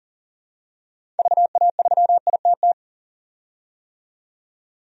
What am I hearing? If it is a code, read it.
Morse code, heard as a keyed tone, about 37 wpm, 700 Hz. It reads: VA3ITT
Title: DXCC-100 Call Signs - 6 Characters - 40wpm